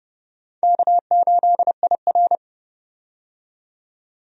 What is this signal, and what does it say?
Morse code, heard as a keyed tone, about 30 wpm, 700 Hz. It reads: K8IR
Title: USA Call Signs - 4 Characters - 30wpm